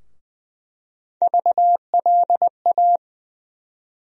Morse code, heard as a keyed tone, about 20 wpm, 700 Hz. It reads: VLA
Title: USA Call Sign Postfixes - 3 Letters 20wpm